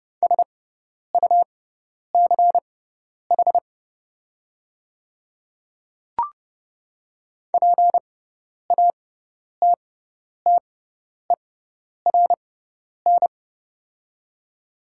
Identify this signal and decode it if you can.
Morse code, heard as a keyed tone, about 30 wpm, spaced as for 10 wpm, 690 Hz. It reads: SUCH PATTERN
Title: 6x Character Spacing - Top 500 Words - 30wpm